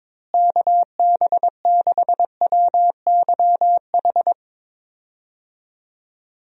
Morse code, heard as a keyed tone, about 22 wpm, 700 Hz. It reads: KB6WYH
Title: USA Call Signs - 6 Characters - Repeated 3x - 22wpm